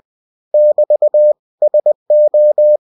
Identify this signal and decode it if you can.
Morse code, heard as a keyed tone, about 20 wpm, 600 Hz. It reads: <BT> SO